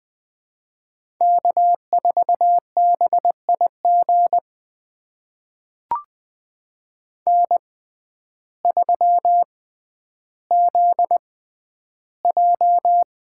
Morse code, heard as a keyed tone, about 21 wpm, spaced as for 11 wpm, 700 Hz. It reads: K4BIG N3ZJ
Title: USA Call Signs - 5 Characters - 6x Character Spacing - 20wpm